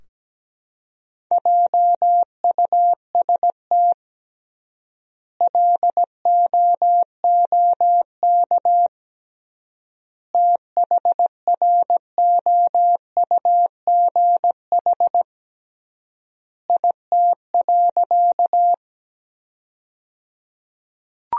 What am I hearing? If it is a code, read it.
Morse code, heard as a keyed tone, about 17 wpm, 700 Hz. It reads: JUST LOOK THROUGH IT.
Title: 3x Word-Spacing - Sentences from Top 200 Words - 17wpm